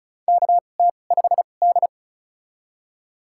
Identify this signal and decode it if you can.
Morse code, heard as a keyed tone, about 36 wpm, spaced as for 25 wpm, 690 Hz. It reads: KT5D
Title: USA Call Signs - 4 Characters - 2x Character Spacing - 35wpm